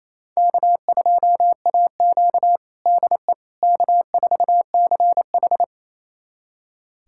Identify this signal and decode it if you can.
Morse code, heard as a keyed tone, about 28 wpm, 700 Hz. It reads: K2AQ DE K4CH
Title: USA Callsign DE Callsign - 4 Characters - 28wpm